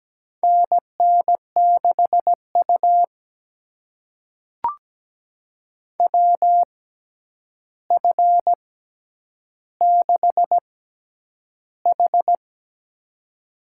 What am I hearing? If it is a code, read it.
Morse code, heard as a keyed tone, about 17 wpm, spaced as for 9 wpm, 700 Hz. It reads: NN6U WF6H
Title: USA Call Signs - 4 Characters - 6x Character Spacing - 17wpm